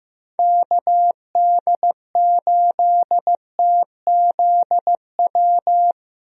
Morse code, heard as a keyed tone, about 15 wpm, 700 Hz. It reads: KD8TZW
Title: USA Call Signs - 6 Characters - Repeated 3x - 15wpm